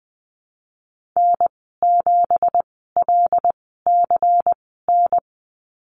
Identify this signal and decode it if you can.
Morse code, heard as a keyed tone, about 20 wpm, spaced as for 15 wpm, 700 Hz. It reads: N7LCN
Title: 179 Lesson 29g - Slash - Call Signs 20wpm